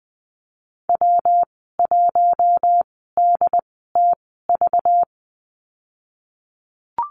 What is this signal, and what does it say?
Morse code, heard as a keyed tone, about 20 wpm, spaced as for 15 wpm, 700 Hz. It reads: W1DTV
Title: 193 Lesson 31g - Letter v - Call Signs 20wpm